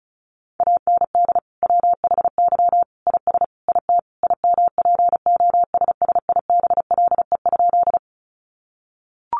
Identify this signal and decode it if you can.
Morse code, heard as a keyed tone, about 34 wpm, 700 Hz. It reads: AND WHY IS IT IMPOSSIBLE?
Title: Sentences from Top 1100 Words 35wpm